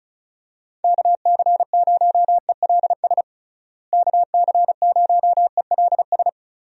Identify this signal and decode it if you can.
Morse code, heard as a keyed tone, about 34 wpm, 700 Hz. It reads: KC0ELS KC0ELS
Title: USA Call Signs - 6 Characters - Repeated 3x - 35wpm